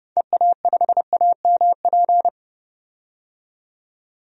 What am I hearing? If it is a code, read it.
Morse code, heard as a keyed tone, about 30 wpm, 700 Hz. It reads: EA5AMP
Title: DXCC-100 Call Signs - 6 Characters - 30wpm